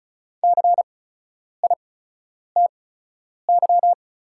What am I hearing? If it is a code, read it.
Morse code, heard as a keyed tone, about 36 wpm, spaced as for 10 wpm, 690 Hz. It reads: CITY
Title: 8x Character Spacing - Top 500 Words - 35wpm